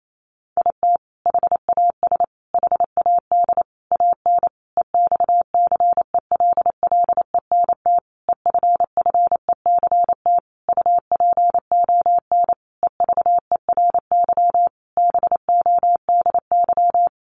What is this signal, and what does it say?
Morse code, heard as a keyed tone, about 28 wpm, 700 Hz. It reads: IT HAS HAD AN EXCELLENT EFFECT UPON EVERY BODY
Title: Sentences from Top 1100 Words 28wpm